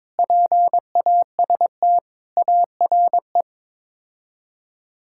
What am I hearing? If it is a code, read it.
Morse code, heard as a keyed tone, about 21 wpm, 700 Hz. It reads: PAST ARE
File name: practice / 22wpm / Sets of 8 Words - 22wpm